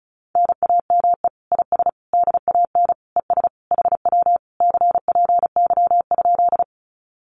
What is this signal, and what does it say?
Morse code, heard as a keyed tone, about 34 wpm, 700 Hz. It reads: NAME IS DAN ES HW CPY?
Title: Common QSO Phrases Encore 35wpm